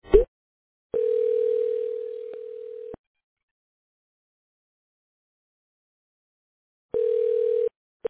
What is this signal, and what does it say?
Ringback tone (North America), 440 + 480 Hz.